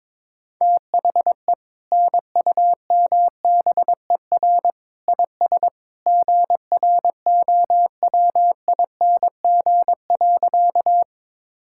Morse code, heard as a keyed tone, about 22 wpm, 700 Hz. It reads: THE NUMBER IS GROWING.
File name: practice / 22wpm / Sentences with 2-Word Phrases - 22wpm